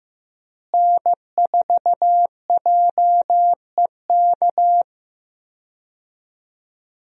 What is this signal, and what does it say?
Morse code, heard as a keyed tone, about 15 wpm, 700 Hz. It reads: N4JEK
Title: USA Call Signs - 5 Characters - Repeated 2x - 15wpm